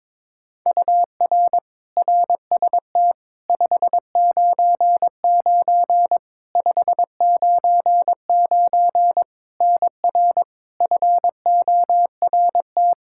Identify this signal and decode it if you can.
Morse code, heard as a keyed tone, about 22 wpm, 700 Hz. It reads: UR RST 599 599 NR FORT
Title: Practice QSO - Exchange 1 - Beginning - 22wpm